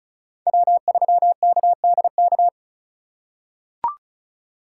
Morse code, heard as a keyed tone, about 36 wpm, 700 Hz. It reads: W3KDK
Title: USA Call Signs - 5 Characters - 3x Character Spacing - 35wpm